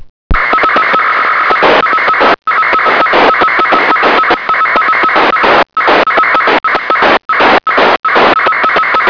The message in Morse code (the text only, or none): none